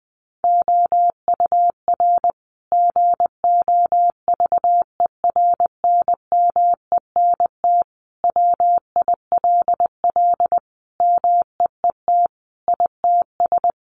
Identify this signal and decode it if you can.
Morse code, heard as a keyed tone, about 20 wpm, 700 Hz. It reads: OUR GOVERNMENT WILL MEET ITS
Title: Sentences from Top 1200 Words 20wpm